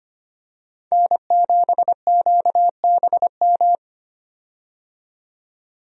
Morse code, heard as a keyed tone, about 25 wpm, 700 Hz. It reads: N7QBM
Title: USA Call Signs - 5 Characters - 25wpm